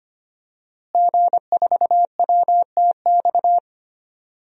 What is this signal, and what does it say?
Morse code, heard as a keyed tone, about 25 wpm, 700 Hz. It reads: G4WTX
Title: DXCC-100 Call Signs - 5 Characters - Repeated 3x - 25wpm